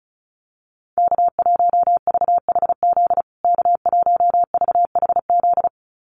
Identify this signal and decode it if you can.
Morse code, heard as a keyed tone, about 34 wpm, 700 Hz. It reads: K1VHZ K1VHZ
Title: USA Call Signs Repeated 2x 35wpm